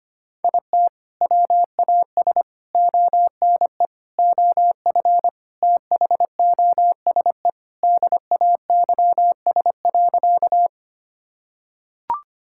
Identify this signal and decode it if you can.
Morse code, heard as a keyed tone, about 25 wpm, 700 Hz. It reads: IT WAS ONE OF THOSE DAYS.
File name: practice / 25wpm / Sentences with 4-Word Phrases - 25wpm